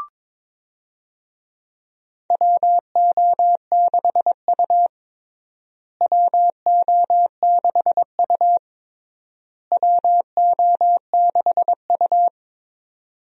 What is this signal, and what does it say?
Morse code, heard as a keyed tone, about 21 wpm, 700 Hz. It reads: WO6U WO6U WO6U
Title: USA Call Signs - 4 Characters - Repeated 3x - 22wpm